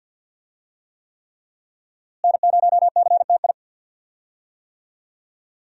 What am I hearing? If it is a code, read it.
Morse code, heard as a keyed tone, about 48 wpm, 700 Hz. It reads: N0CTI
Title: USA Call Signs - 5 Characters - Repeated 3x - 50wpm